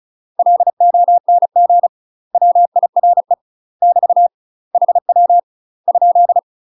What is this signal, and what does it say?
Morse code, heard as a keyed tone, about 34 wpm, 700 Hz. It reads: LONG WIRE <BT> HW ?
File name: UK MASTER QSOs TRAINING / 35WPM / UK MASTER QSOs 0009